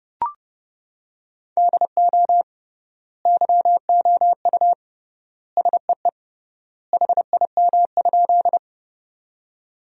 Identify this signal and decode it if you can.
Morse code, heard as a keyed tone, about 31 wpm, 700 Hz. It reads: DO YOU SEE HIM?
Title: 3x Word-Spacing - Sentences from Top 100 Words - 30wpm